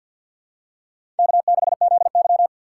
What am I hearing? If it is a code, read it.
Morse code, heard as a keyed tone, about 51 wpm, 700 Hz. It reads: K6ZY
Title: USA Call Signs - 4 Characters - 3x Character Spacing - 50wpm